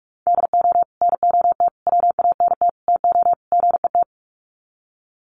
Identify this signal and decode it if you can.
Morse code, heard as a keyed tone, about 44 wpm, 700 Hz. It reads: DO NOT WANT TO GET
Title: Sets of 5 Words 45wpm